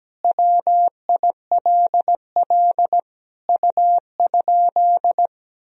Morse code, heard as a keyed tone, about 17 wpm, 700 Hz. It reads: WILL U?
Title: Sentences from Top 300 Words and Abbreviations 17wpm